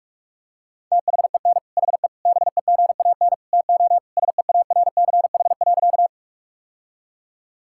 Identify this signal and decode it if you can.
Morse code, heard as a keyed tone, about 46 wpm, 700 Hz. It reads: THEN HE BEGAN TO SEARCH.
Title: Sentences from Top 1000 Words - Review Part 5 - 45wpm